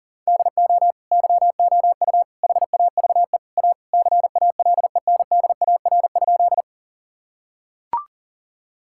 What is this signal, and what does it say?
Morse code, heard as a keyed tone, about 40 wpm, 700 Hz. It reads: DO YOU HAVE A CALENDAR?
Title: Sentences from Top 1600 Words 40wpm